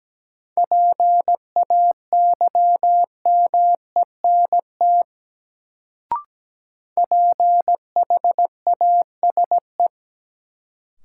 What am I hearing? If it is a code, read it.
Morse code, heard as a keyed tone, about 17 wpm, 700 Hz. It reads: PAYMENT PHASE
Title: Sentences from Top 1200 Words - New Word Drill Alt - 17wpm